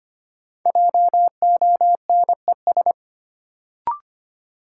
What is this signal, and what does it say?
Morse code, heard as a keyed tone, about 25 wpm, 700 Hz. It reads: JONES
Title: Sets of 5 Words - New Word Drill - 25wpm